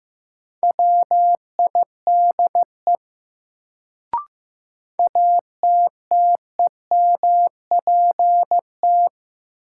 Morse code, heard as a keyed tone, about 15 wpm, 700 Hz. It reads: WIDE ATTEMPT
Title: Sets of 4 Words - New Word Drill - 15wpm